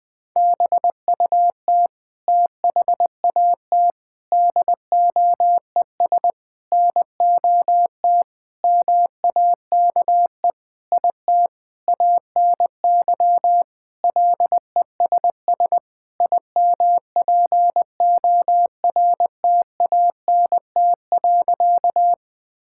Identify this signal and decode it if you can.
Morse code, heard as a keyed tone, about 21 wpm, 700 Hz. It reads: BUT THAT DOES NOT MAKE IT ANY LESS IMPORTANT.
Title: Sentences with 7-Word Phrases - Spoken Beginning - 20wpm